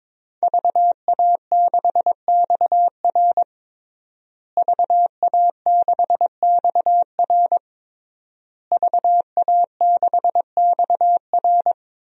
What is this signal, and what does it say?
Morse code, heard as a keyed tone, about 22 wpm, 700 Hz. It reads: VA6XR VA6XR VA6XR
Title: DXCC-100 Call Signs - 5 Characters - Repeated 3x - 22wpm